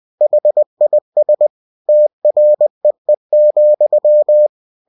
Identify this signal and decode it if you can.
Morse code, heard as a keyed tone, about 20 wpm, 600 Hz. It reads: HIS TREE,
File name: cw-books / winnie-the-pooh / winnie-20wpm-page-0094